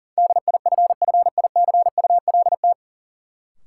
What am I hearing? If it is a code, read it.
Morse code, heard as a keyed tone, about 37 wpm, 700 Hz. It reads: DIFFICULT
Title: Sets of 4 Words - New Word Drill - 40wpm